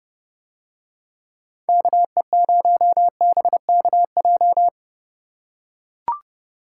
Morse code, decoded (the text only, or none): KE0BKJ